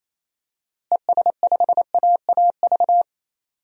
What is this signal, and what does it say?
Morse code, heard as a keyed tone, about 28 wpm, 700 Hz. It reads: ES5AAV
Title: DXCC-100 Call Signs - 6 Characters - Repeated 3x - 28wpm